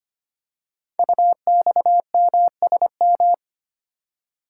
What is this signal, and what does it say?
Morse code, heard as a keyed tone, about 25 wpm, 700 Hz. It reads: UXMSM
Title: Code Groups - 5 Letters - 25wpm